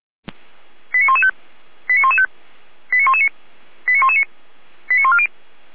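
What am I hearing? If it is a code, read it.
Five-tone selective call (ZVEI), heard as quick runs of five tones, tones of about 70 ms. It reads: 89115 89115 89119 89119 89140
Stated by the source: Received on an SDR, ZVEI five-tone call.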